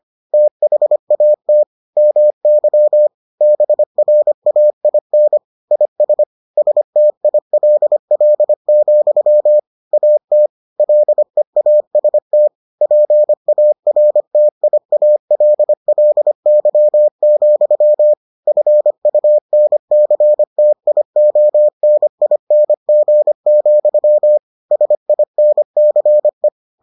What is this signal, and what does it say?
Morse code, heard as a keyed tone, about 25 wpm, 600 Hz. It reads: THAT MY BRAIN IS STILL, AT LEAST PARTIALLY, FUNCTIONING, SINCE